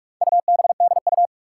Morse code, heard as a keyed tone, about 43 wpm, 700 Hz. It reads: UBDU
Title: Four Challenging Letter-Number Combinations 45wpm